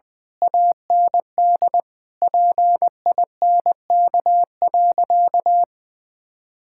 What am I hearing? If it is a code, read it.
Morse code, heard as a keyed tone, about 20 wpm, 700 Hz. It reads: AND PINK.
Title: Difficult Sentences with Hints - 20wpm